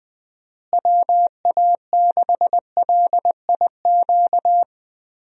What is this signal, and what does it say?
Morse code, heard as a keyed tone, about 21 wpm, 700 Hz. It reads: WA6LIQ
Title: USA Call Signs - 6 Characters - Repeated 2x - 20wpm